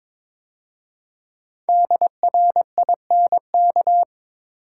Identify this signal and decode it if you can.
Morse code, heard as a keyed tone, about 23 wpm, 700 Hz. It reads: DRINK
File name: 5 Letter Words - First 3 Letters Given - 22wpm